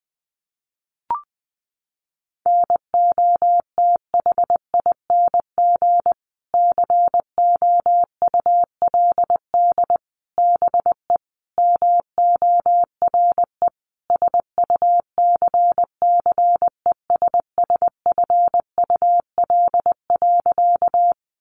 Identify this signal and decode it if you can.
Morse code, heard as a keyed tone, about 20 wpm, 700 Hz. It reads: NOTHING COULD BE MORE SUCCESSFUL.
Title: Sentences from Top 900 Words 20wpm